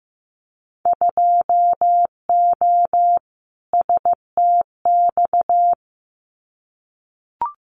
Morse code, heard as a keyed tone, about 15 wpm, 700 Hz. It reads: 2O STX
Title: Winter Field Day Exchanges 15wpm